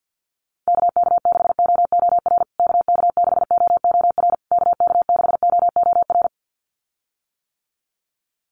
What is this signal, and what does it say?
Morse code, heard as a keyed tone, about 50 wpm, 700 Hz. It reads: KK6OOR KK6OOR KK6OOR
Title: USA Call Signs Repeated 3x 50wpm